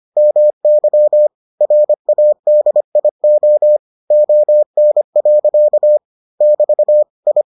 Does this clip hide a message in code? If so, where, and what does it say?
Yes: Morse code from 0.2 to 7.4 s, heard as a keyed tone, about 25 wpm, 600 Hz. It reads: MY RADIO ON. <BT> I